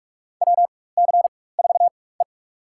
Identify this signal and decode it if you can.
Morse code, heard as a keyed tone, about 46 wpm, spaced as for 21 wpm, 700 Hz. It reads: WC4E
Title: USA Call Signs - 4 Characters - 4x Character Spacing - 45wpm